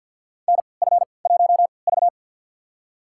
Morse code, heard as a keyed tone, about 51 wpm, spaced as for 28 wpm, 700 Hz. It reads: NF1V
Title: USA Call Signs - 4 Characters - 3x Character Spacing - 50wpm